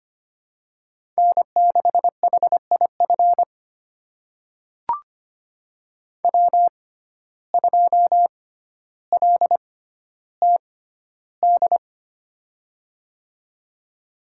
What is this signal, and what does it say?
Morse code, heard as a keyed tone, about 25 wpm, spaced as for 13 wpm, 700 Hz. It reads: N6HIF W2LTD
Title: USA Call Signs - 5 Characters - 6x Character Spacing - 25wpm